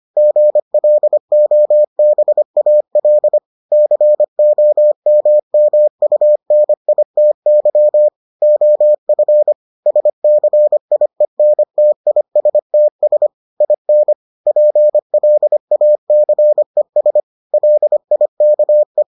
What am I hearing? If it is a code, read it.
Morse code, heard as a keyed tone, about 25 wpm, 600 Hz. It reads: GLOBAL COMMUNITY OF SCIENTISTS IN PLACES LIKE